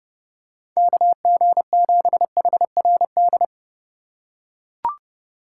Morse code, heard as a keyed tone, about 31 wpm, 700 Hz. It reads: KG7HRD